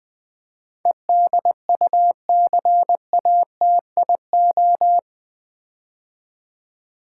Morse code, heard as a keyed tone, about 20 wpm, 700 Hz. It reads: EDUCATIO
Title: Build A Word - 4 Letters At A Time - 11-Letter Words - 20wpm